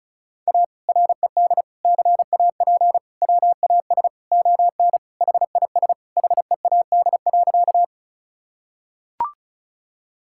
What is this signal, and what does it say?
Morse code, heard as a keyed tone, about 34 wpm, 700 Hz. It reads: A RED CAP WAS ON HIS HEAD.
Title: Sentences from Top 1600 Words 35wpm